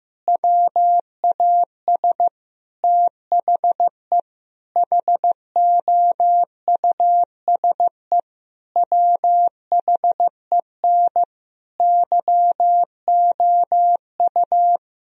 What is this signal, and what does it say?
Morse code, heard as a keyed tone, about 15 wpm, 700 Hz. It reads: WAS THE HOUSE WHEN YOU
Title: Sentences from Top 1000 Words - Review Part 2 - 15wpm